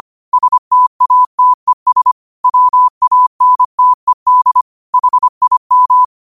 Morse code, heard as a keyed tone, about 25 wpm, 1000 Hz. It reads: STATES WANTED HIM
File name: qotdfiles / 25wpm